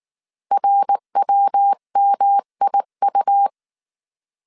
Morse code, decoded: RWMIU

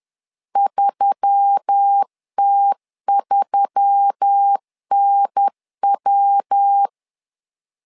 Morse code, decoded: 3T3NW

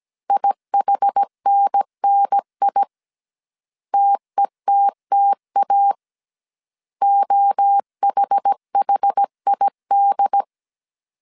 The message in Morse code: IHNNI TETTA OHHID

